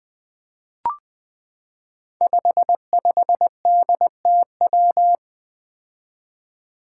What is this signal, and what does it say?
Morse code, heard as a keyed tone, about 20 wpm, 700 Hz. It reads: 55DTW
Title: Five Challenging Letter-Number Combinations 20wpm